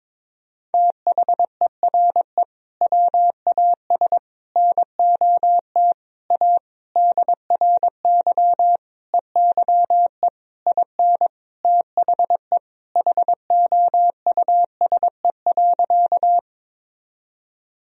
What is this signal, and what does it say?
Morse code, heard as a keyed tone, about 22 wpm, 700 Hz. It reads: THERE WAS NOT A DRY EYE IN THE HOUSE.
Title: Sentences from Top 1000 Words - Review Part 3 - 22wpm